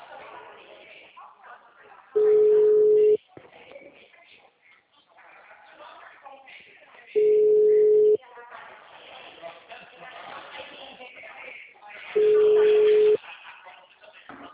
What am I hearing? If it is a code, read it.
Ringback tone (Europe, 425 Hz), with speech.
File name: out- -2010- - - .12983